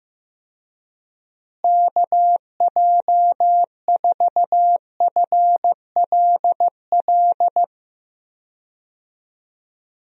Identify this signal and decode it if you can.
Morse code, heard as a keyed tone, about 15 wpm, 700 Hz. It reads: KJ4FLL
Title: USA Call Signs - 6 Characters - Repeated 3x - 15wpm